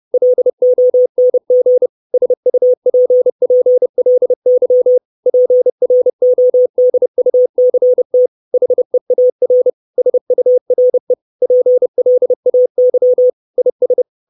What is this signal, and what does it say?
Morse code, heard as a keyed tone, about 30 wpm, 500 Hz. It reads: LONG SUPPLY PRODUCT HEAR SURE PLAY IS
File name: 632 Words 30WPM